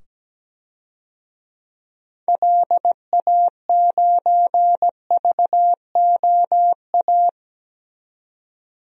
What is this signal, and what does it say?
Morse code, heard as a keyed tone, about 17 wpm, 700 Hz. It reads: LA9VOA
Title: DXCC-100 Call Signs - 6 Characters - 17wpm